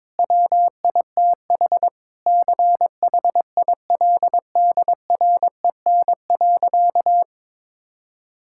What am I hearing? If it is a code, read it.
Morse code, heard as a keyed tone, about 22 wpm, 700 Hz. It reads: WITH CHILDREN.
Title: Sentences from Top 1000 Words - Review Part 5 - 22wpm